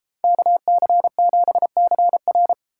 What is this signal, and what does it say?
Morse code, heard as a keyed tone, about 31 wpm, 700 Hz. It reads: KC7CR